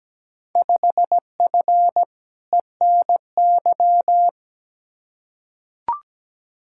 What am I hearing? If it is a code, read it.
Morse code, heard as a keyed tone, about 17 wpm, 700 Hz. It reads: 5F ENY
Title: Field Day Exchanges Encore - 17wpm